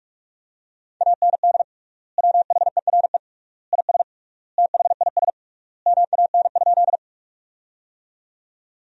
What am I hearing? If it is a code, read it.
Morse code, heard as a keyed tone, about 48 wpm, 700 Hz. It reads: AND WHERE IS THIS MAN?
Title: 3x Word-Spacing - Sentences from Top 200 Words - 45wpm